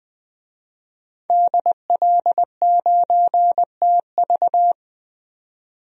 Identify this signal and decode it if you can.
Morse code, heard as a keyed tone, about 21 wpm, 700 Hz. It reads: DL9TV
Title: DXCC-100 Call Signs - 5 Characters - 20wpm